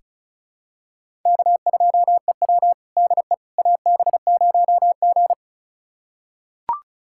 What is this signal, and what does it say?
Morse code, heard as a keyed tone, about 36 wpm, 690 Hz. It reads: K2EW DE AB0G
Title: USA Callsign DE Callsign - 4 Characters - 35wpm